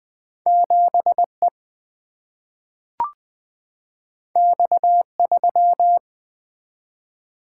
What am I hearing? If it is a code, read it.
Morse code, heard as a keyed tone, about 20 wpm, 700 Hz. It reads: 7 E X 3
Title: Single-Letter Single-Number 20wpm